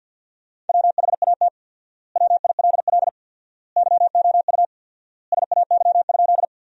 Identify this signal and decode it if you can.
Morse code, heard as a keyed tone, about 50 wpm, 700 Hz. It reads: WHAT WILL YOU SAY?
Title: 4x Word-Spacing - Sentences from Top 200 Words - 50wpm